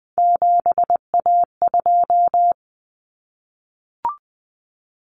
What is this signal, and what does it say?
Morse code, heard as a keyed tone, about 20 wpm, 700 Hz. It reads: 7A2
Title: Common Cut Numbers - 3-Digit 20wpm